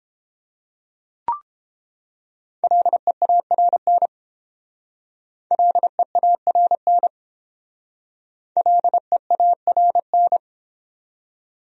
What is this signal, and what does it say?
Morse code, heard as a keyed tone, about 29 wpm, 700 Hz. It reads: LEARN LEARN LEARN